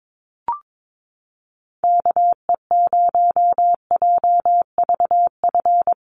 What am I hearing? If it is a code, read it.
Morse code, heard as a keyed tone, about 22 wpm, 700 Hz. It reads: KE0JVF
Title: USA Call Signs Repeated 2x 22wpm